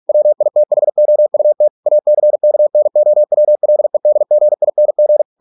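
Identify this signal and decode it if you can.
Morse code, heard as a keyed tone, about 46 wpm, 600 Hz. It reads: WITHOUT ACKNOWLEDGING
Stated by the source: Morse practice at 45 wpm, the novel Never by Ken Follett in English, from CW med Gnister.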